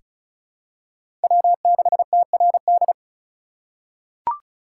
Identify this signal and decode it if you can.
Morse code, heard as a keyed tone, about 36 wpm, 700 Hz. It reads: W6TRD
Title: USA Call Signs - 5 Characters - 4x Character Spacing - 35wpm